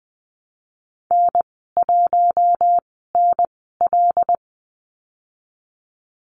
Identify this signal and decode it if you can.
Morse code, heard as a keyed tone, about 20 wpm, spaced as for 15 wpm, 700 Hz. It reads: N1NL
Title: 122 Lesson 20g - Number 3 - Call Signs 20wpm